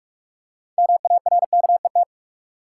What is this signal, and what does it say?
Morse code, heard as a keyed tone, about 44 wpm, 700 Hz. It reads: MARKET
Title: Six Letter Words - 45wpm